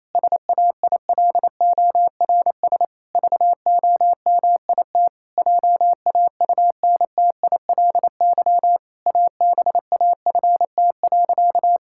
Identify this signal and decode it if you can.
Morse code, heard as a keyed tone, about 28 wpm, 700 Hz. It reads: SAILORS VOMIT JAUNTILY ABAFT.
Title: Pangram Challenge v2 - 28wpm